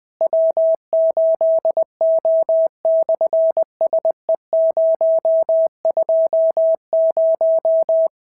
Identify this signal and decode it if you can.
Morse code, heard as a keyed tone, about 20 wpm, 650 Hz. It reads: W8O/SE020
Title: SOTA QSO Practice 20wpm